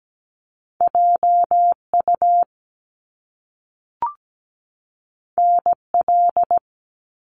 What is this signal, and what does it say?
Morse code, heard as a keyed tone, about 17 wpm, 700 Hz. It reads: JU NL
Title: Top Two Letter Combinations in Words 17wpm